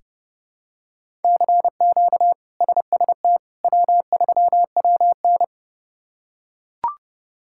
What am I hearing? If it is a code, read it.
Morse code, heard as a keyed tone, about 32 wpm, 700 Hz. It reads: CQ SST W3WN
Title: K1USN SST QSO Practice - 30wpm